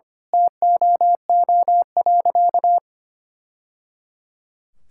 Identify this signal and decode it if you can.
Morse code, heard as a keyed tone, about 25 wpm, 700 Hz. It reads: TOO.
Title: Sentences from Top 1000 Words - Review Part 6 - 25wpm